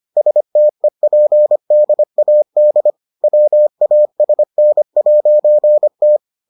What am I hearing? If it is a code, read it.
Morse code, heard as a keyed tone, about 25 wpm, 600 Hz. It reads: STEPDAD WASN'T